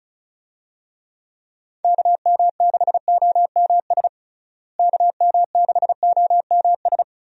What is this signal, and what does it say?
Morse code, heard as a keyed tone, about 34 wpm, 700 Hz. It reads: KM6OMS KM6OMS
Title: USA Call Signs - 6 Characters - Repeated 2x - 35wpm